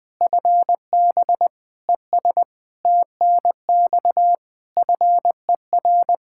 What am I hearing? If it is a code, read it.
Morse code, heard as a keyed tone, about 21 wpm, 700 Hz. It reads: FB ES TNX FER
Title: K1USN SST QSO Practice 20wpm